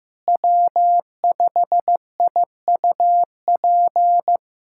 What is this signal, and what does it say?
Morse code, heard as a keyed tone, about 15 wpm, 700 Hz. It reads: W5IUP
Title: USA Callsign DE Callsign - 5 Characters - 15wpm